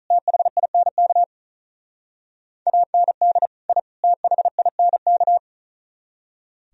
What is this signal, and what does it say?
Morse code, heard as a keyed tone, about 44 wpm, 700 Hz. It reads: THINK AND I THINK